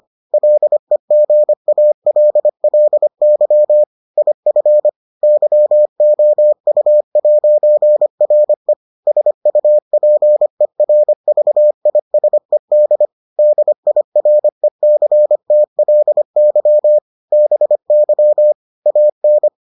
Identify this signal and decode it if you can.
Morse code, heard as a keyed tone, about 25 wpm, 600 Hz. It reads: LEGALLY IF YOU'RE SUPERVISED DIRECTLY BY AN